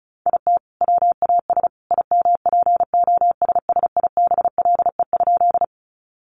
Morse code, heard as a keyed tone, about 34 wpm, 700 Hz. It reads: IT WAS IMPOSSIBLE?
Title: Sentences from Top 1100 Words 35wpm